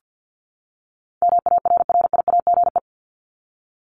Morse code, heard as a keyed tone, about 50 wpm, 700 Hz. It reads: MARRIAGE
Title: Top 751-1000 Words 50wpm